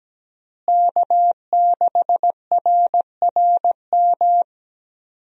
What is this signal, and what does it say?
Morse code, heard as a keyed tone, about 17 wpm, 700 Hz. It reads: K6RRM
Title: USA Call Signs - 5 Characters - Repeated 2x - 17wpm